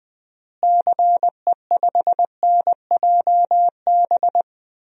Morse code, heard as a keyed tone, about 21 wpm, 700 Hz. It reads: CE5NJB
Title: DXCC-100 Call Signs - 6 Characters - Repeated 3x - 20wpm